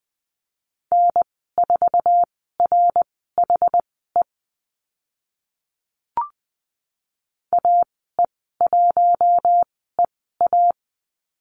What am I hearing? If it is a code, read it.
Morse code, heard as a keyed tone, about 20 wpm, spaced as for 14 wpm, 700 Hz. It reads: N4RHE AE1EA
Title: 082 Lesson 14g - Number 2 - Call Signs 20wpm